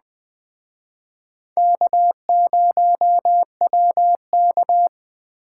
Morse code, heard as a keyed tone, about 20 wpm, 700 Hz. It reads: K0WK
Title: Speed Racing - USA 1x2 Call Signs 17wpm